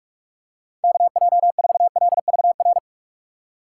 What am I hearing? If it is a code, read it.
Morse code, heard as a keyed tone, about 47 wpm, 700 Hz. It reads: KJ4LVR
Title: USA Call Signs - 6 Characters - 6x Character Spacing - 45wpm